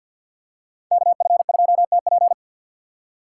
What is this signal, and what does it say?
Morse code, heard as a keyed tone, about 50 wpm, 700 Hz. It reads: KF2TP
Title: K1USN SST QSO Practice - 50wpm